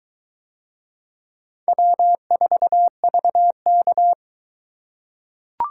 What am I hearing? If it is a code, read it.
Morse code, heard as a keyed tone, about 22 wpm, 700 Hz. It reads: W4VK